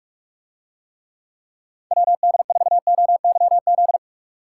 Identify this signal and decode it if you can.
Morse code, heard as a keyed tone, about 47 wpm, 700 Hz. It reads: WD4OYZ